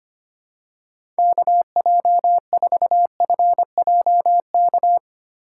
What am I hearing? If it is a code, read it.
Morse code, heard as a keyed tone, about 25 wpm, 700 Hz. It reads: KJ4FJK
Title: USA Call Signs - 6 Characters - 6x Character Spacing - 25wpm